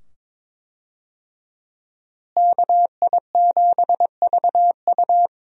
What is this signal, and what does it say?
Morse code, heard as a keyed tone, about 22 wpm, 700 Hz. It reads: KI7VU
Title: USA Call Signs - 5 Characters - Repeated 2x - 22wpm